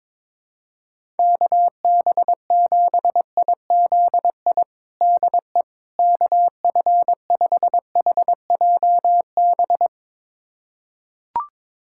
Morse code, heard as a keyed tone, about 22 wpm, 700 Hz. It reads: KB7IZI DE KF5HJB
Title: USA Callsign DE Callsign - 6 Characters - 22wpm